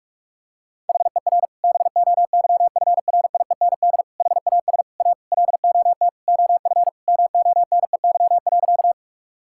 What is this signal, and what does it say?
Morse code, heard as a keyed tone, about 44 wpm, 690 Hz. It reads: HER BOYFRIEND HAS A LOT OF MONEY.
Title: Sentences from Top 1000 Words - Review Part 5 - 45wpm